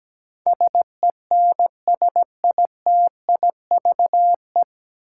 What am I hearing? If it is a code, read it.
Morse code, heard as a keyed tone, about 17 wpm, 700 Hz. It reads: SENSITIVE
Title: Sentences from Top 1400 Words - New Word Drill Alt - 17wpm